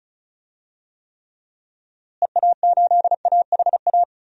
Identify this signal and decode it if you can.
Morse code, heard as a keyed tone, about 36 wpm, 700 Hz. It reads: EA8AHA
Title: DXCC-100 Call Signs - 6 Characters - Repeated 2x - 35wpm